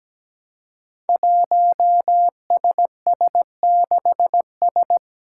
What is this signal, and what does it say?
Morse code, heard as a keyed tone, about 17 wpm, 700 Hz. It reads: 1SS6S
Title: Five Challenging Letter-Number Combinations 17wpm